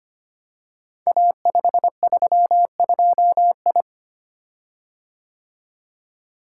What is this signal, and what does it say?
Morse code, heard as a keyed tone, about 25 wpm, 700 Hz. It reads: A532I
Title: Code Groups - 5 Letter:Numbers - 25wpm